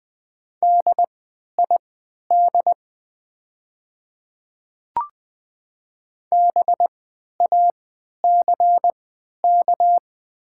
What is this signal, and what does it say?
Morse code, heard as a keyed tone, about 20 wpm, spaced as for 11 wpm, 690 Hz. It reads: DID BACK
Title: 3x Character Spacing - Top 200 Words - 20wpm